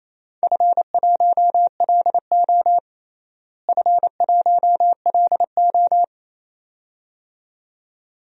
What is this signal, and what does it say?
Morse code, heard as a keyed tone, about 28 wpm, 700 Hz. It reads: F1LO F1LO
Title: DXCC-100 Call Signs - 4 Characters - Repeated 2x - 28wpm